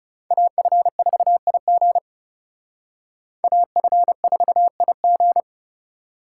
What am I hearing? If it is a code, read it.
Morse code, heard as a keyed tone, about 38 wpm, 700 Hz. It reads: AF4IG AF4IG